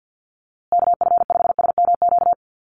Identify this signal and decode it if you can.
Morse code, heard as a keyed tone, about 50 wpm, 700 Hz. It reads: KF5SMQ
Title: USA Call Signs Repeated 3x 50wpm